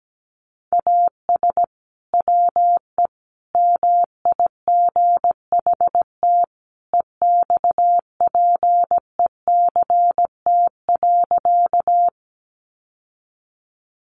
Morse code, heard as a keyed tone, about 17 wpm, 700 Hz. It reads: AS WE MIGHT EXPECT.
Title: Sentences from Top 1100 Words 17wpm